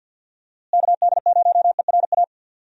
Morse code, heard as a keyed tone, about 46 wpm, 700 Hz. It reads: KD0ERA
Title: USA Call Signs - 6 Characters - Repeated 2x - 50wpm